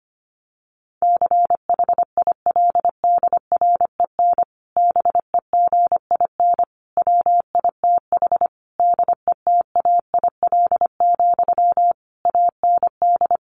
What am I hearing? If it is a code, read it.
Morse code, heard as a keyed tone, about 25 wpm, 700 Hz. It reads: CHILDREN BEGIN WITH DETAIL, AND
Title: Sentences from Top 1500 Words 25wpm